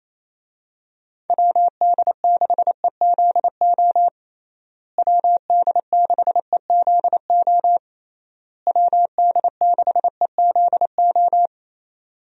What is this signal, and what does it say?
Morse code, heard as a keyed tone, about 28 wpm, 700 Hz. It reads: WD6EZO WD6EZO WD6EZO
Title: USA Call Signs - 6 Characters - Repeated 3x - 28wpm